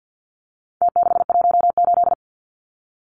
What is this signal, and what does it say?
Morse code, heard as a keyed tone, about 49 wpm, 700 Hz. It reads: T618
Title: Numbers - Contest Style - 4-Digit 50wpm